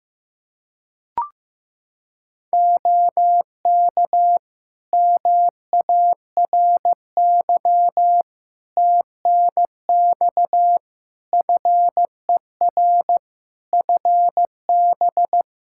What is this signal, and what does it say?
Morse code, heard as a keyed tone, about 15 wpm, 700 Hz. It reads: OK MARY TNX FER FB
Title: Practice QSO - Exchange 4 - Ending - 15wpm